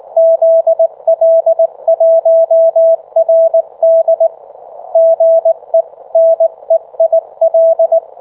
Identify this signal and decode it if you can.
Morse code, heard as a keyed tone, about 22 wpm, 660 Hz. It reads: ZL1RD GE NEIL